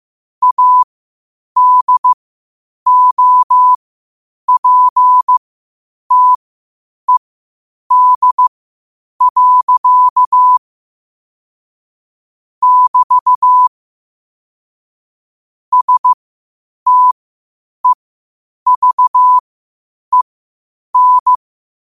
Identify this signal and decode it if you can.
Morse code, heard as a keyed tone, about 15 wpm, spaced as for 8 wpm, 1000 Hz. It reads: ADOPTED. <BT> STEVEN
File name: qotdfiles / 8wpm